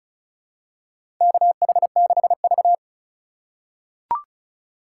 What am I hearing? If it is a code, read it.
Morse code, heard as a keyed tone, about 36 wpm, 700 Hz. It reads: KH6V
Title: USA Call Signs - 4 Characters - 2x Character Spacing - 35wpm